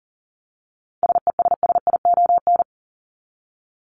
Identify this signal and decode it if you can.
Morse code, heard as a keyed tone, about 40 wpm, 700 Hz. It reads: SESSION
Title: Top 1501-1750 Words 40wpm